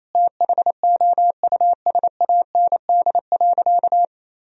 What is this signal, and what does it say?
Morse code, heard as a keyed tone, about 28 wpm, 700 Hz. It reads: THOUSAND.
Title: Sentences from Top 1000 Words - Review Part 3 - 28wpm